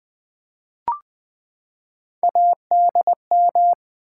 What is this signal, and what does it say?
Morse code, heard as a keyed tone, about 20 wpm, 700 Hz. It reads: ADM
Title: Build A Word - 3 Letters At A Time - 9-Letter Words - 20wpm